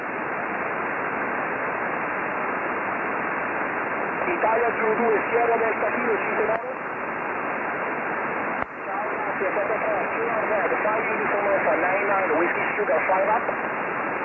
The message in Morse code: none